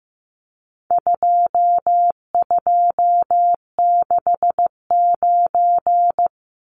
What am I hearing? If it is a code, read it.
Morse code, heard as a keyed tone, about 15 wpm, 700 Hz. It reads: 2269
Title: Numbers - 4-Digit 15wpm